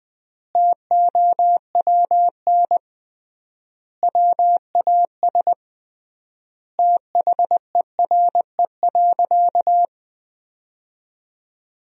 Morse code, heard as a keyed tone, about 20 wpm, 700 Hz. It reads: TOWN WAS THERE.
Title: 3x Word-Spacing - Sentences from Top 500 Words - 20wpm